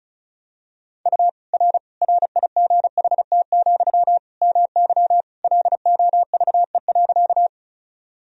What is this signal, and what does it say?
Morse code, heard as a keyed tone, about 36 wpm, 700 Hz. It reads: U R RIGHT, MY LOVE.
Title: Sentences from Top 400 Words and Abbreviations 35wpm